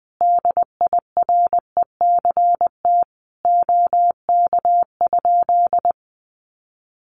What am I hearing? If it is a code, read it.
Morse code, heard as a keyed tone, about 20 wpm, 700 Hz. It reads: DIRECT OK?
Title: Common QSO Phrases 20wpm